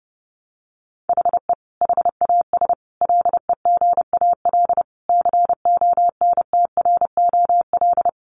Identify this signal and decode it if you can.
Morse code, heard as a keyed tone, about 30 wpm, 700 Hz. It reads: HE HAS LEGAL CONTROL
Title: Sentences from Top 1200 Words 30wpm